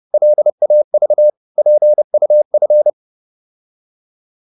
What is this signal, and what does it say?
Morse code, heard as a keyed tone, about 30 wpm, 600 Hz. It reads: LAV PUF
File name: Korte ord DK 30wpm